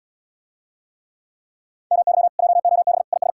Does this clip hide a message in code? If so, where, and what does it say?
Yes: Morse code from 1.9 to 3.3 s, heard as a keyed tone, about 80 wpm, 700 Hz. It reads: MY JOB IS